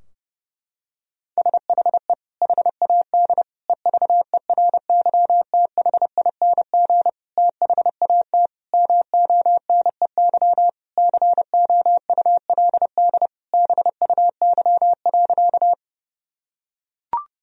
Morse code, heard as a keyed tone, about 30 wpm, 700 Hz. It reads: SHE HAD EVERYTHING THAT MONEY COULD BUY.
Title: Sentences from Top 1000 Words - Review Part 4 - 30wpm